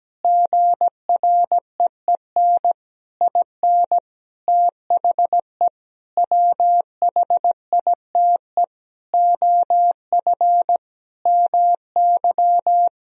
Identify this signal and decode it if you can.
Morse code, heard as a keyed tone, about 17 wpm, 700 Hz. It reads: GREEN IN THE WHITE OF MY
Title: Sentences from Top 500 Words and Abbreviations 17wpm